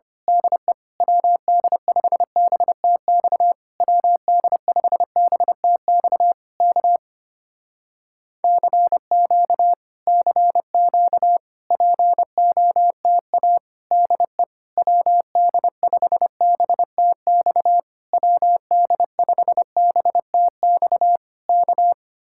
Morse code, heard as a keyed tone, about 25 wpm, 700 Hz. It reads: DE WD5BTX WD5BTX K CQ CQ POTA DE WD5BTX WD5BTX K
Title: Speed-Racing Common QSO Phrases - 25wpm